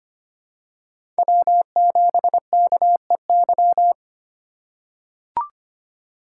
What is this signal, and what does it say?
Morse code, heard as a keyed tone, about 25 wpm, 700 Hz. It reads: W7KEY
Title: USA Call Signs - 5 Characters - 2x Character Spacing - 25wpm